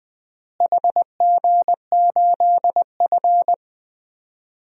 Morse code, heard as a keyed tone, about 20 wpm, 700 Hz. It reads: HG8F
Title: DXCC-100 Call Signs - 4 Characters - Repeated 2x - 20wpm